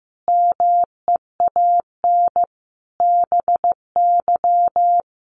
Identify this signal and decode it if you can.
Morse code, heard as a keyed tone, about 15 wpm, 700 Hz. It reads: MEAN BY
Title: Sentences from Top 200 Words 15wpm